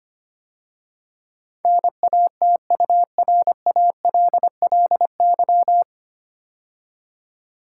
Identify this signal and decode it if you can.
Morse code, heard as a keyed tone, about 25 wpm, 700 Hz. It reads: NATURALLY
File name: practice / 25wpm / 9 Letter Words - First 3 Letters Given - 25wpm